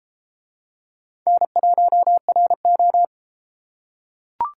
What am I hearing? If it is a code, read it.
Morse code, heard as a keyed tone, about 33 wpm, 700 Hz. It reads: N1RO